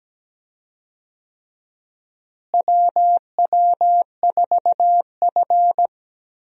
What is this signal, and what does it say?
Morse code, heard as a keyed tone, about 17 wpm, 700 Hz. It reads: WW4F
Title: USA Call Signs - 4 Characters - Repeated 3x - 17wpm